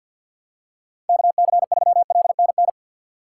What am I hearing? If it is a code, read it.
Morse code, heard as a keyed tone, about 47 wpm, 700 Hz. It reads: KC3LNN
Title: USA Call Signs - 6 Characters - 3x Character Spacing - 50wpm